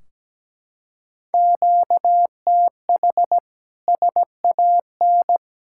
Morse code, heard as a keyed tone, about 17 wpm, 700 Hz. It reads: QTH SAN
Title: Practice QSO - Exchange 1 - Beginning - 17wpm